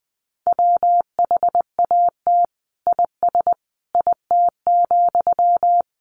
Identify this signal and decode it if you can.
Morse code, heard as a keyed tone, about 20 wpm, 700 Hz. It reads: WHAT IS IT,
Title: Sentences from Top 1200 Words 20wpm